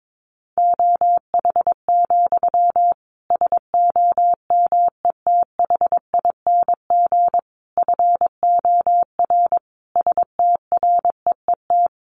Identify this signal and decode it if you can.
Morse code, heard as a keyed tone, about 22 wpm, 700 Hz. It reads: OH, SOMETHING FOR STREET
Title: Sentences from Top 1500 Words 22wpm